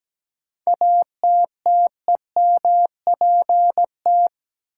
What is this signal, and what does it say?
Morse code, heard as a keyed tone, about 17 wpm, 700 Hz. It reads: ATTEMPT
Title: Sentences from Top 1000 Words - New Word Drill - 17wpm